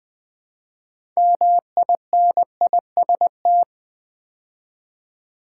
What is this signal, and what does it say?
Morse code, heard as a keyed tone, about 20 wpm, 700 Hz. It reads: MINIST
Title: Build A Word - 2 Letters At A Time - 8-Letter Words - 20wpm